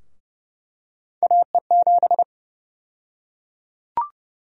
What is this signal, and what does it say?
Morse code, heard as a keyed tone, about 30 wpm, 700 Hz. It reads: AE7
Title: USA Call Sign Prefixes - 30wpm